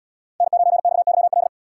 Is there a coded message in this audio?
Yes: Morse code from 0.4 to 1.5 s, heard as a keyed tone, about 72 wpm, 700 Hz. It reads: N9XYL